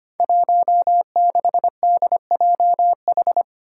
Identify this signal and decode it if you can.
Morse code, heard as a keyed tone, about 25 wpm, 700 Hz. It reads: 16DJH
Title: Five Challenging Letter-Number Combinations 25wpm